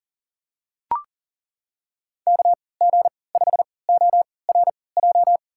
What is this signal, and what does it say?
Morse code, heard as a keyed tone, about 40 wpm, spaced as for 23 wpm, 700 Hz. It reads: KG5ORJ
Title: USA Call Signs - 6 Characters - 3x Character Spacing - 40wpm